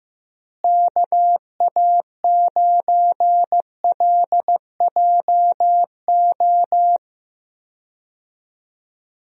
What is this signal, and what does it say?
Morse code, heard as a keyed tone, about 16 wpm, 700 Hz. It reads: KA9LJO
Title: USA Call Signs - 6 Characters - Repeated 2x - 15wpm